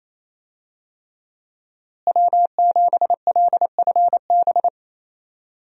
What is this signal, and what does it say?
Morse code, heard as a keyed tone, about 28 wpm, 700 Hz. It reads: W7LFB
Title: USA Call Signs - 5 Characters - Repeated 3x - 28wpm